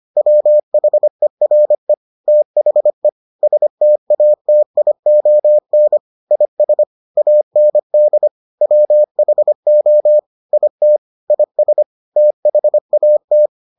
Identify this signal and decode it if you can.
Morse code, heard as a keyed tone, about 25 wpm, 600 Hz. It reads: WHERE THE STATION IS AND WHO IT IS THAT